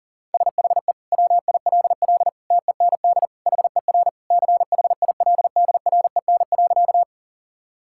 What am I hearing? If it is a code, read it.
Morse code, heard as a keyed tone, about 40 wpm, 700 Hz. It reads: SHE WILL TEND HER CHILDREN.
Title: Sentences from Top 1000 Words - Review Part 5 - 40wpm